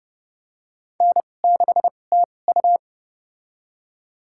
Morse code, heard as a keyed tone, about 30 wpm, spaced as for 22 wpm, 700 Hz. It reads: N6TU
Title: USA Call Signs - 2x Character Spacing - 30wpm